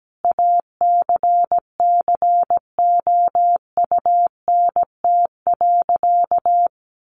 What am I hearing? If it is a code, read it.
Morse code, heard as a keyed tone, about 17 wpm, 700 Hz. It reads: ACCOUNT.
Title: Sentences from Top 1300 Words 17wpm